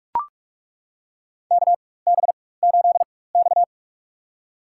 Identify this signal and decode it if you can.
Morse code, heard as a keyed tone, about 41 wpm, spaced as for 20 wpm, 700 Hz. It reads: KB8X
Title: USA Call Signs - 4x Character Spacing - 45wpm